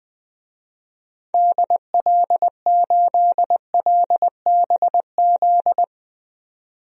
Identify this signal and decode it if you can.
Morse code, heard as a keyed tone, about 20 wpm, 700 Hz. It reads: DL8LBZ
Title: DXCC-100 Call Signs - 6 Characters - 20wpm